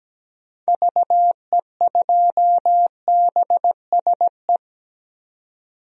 Morse code, heard as a keyed tone, about 17 wpm, 700 Hz. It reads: VE2BSE
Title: DXCC-100 Call Signs - 6 Characters - 17wpm